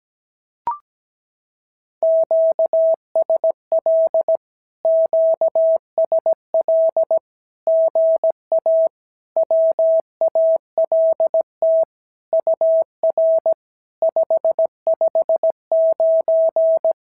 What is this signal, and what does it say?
Morse code, heard as a keyed tone, about 17 wpm, 650 Hz. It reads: QSL QSL GA WALT UR 559
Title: SOTA QSO Practice 17wpm